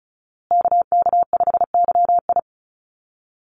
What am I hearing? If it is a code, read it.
Morse code, heard as a keyed tone, about 34 wpm, 700 Hz. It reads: KK5YI
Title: USA Call Signs Repeated 2x 35wpm